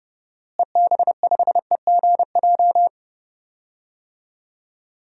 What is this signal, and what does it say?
Morse code, heard as a keyed tone, about 29 wpm, 700 Hz. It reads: EB5EGJ
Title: DXCC-100 Call Signs - 6 Characters - Repeated 2x - 30wpm